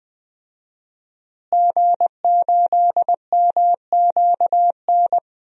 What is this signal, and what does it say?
Morse code, heard as a keyed tone, about 20 wpm, 700 Hz. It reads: G8MQN
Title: DXCC-100 Call Signs - 5 Characters - Repeated 3x - 20wpm